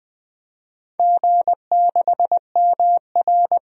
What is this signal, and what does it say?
Morse code, heard as a keyed tone, about 20 wpm, 700 Hz. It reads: G6MR
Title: DXCC-100 Call Signs - 4 Characters - 20wpm